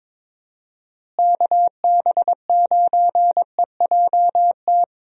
Morse code, heard as a keyed tone, about 22 wpm, 700 Hz. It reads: KB9EJT
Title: USA Call Signs - 6 Characters - Repeated 3x - 22wpm